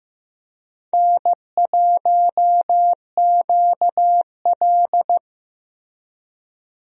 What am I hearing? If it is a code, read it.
Morse code, heard as a keyed tone, about 16 wpm, 700 Hz. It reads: N1QL
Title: USA Call Signs - 4 Characters - Repeated 2x - 15wpm